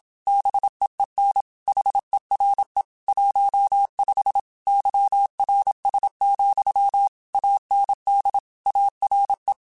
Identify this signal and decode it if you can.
Morse code, heard as a keyed tone, about 27 wpm, 800 Hz. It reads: BEEN HERE 15 YRS, AND ARE